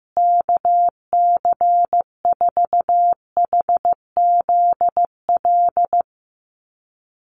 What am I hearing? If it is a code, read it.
Morse code, heard as a keyed tone, about 15 wpm, 700 Hz. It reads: KC4HZL
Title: USA Call Signs Repeated 3x 15wpm